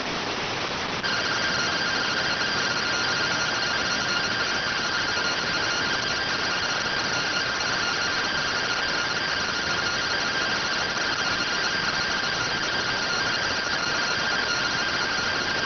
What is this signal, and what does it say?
Two-tone teleprinter signal, about 170 Hz shift.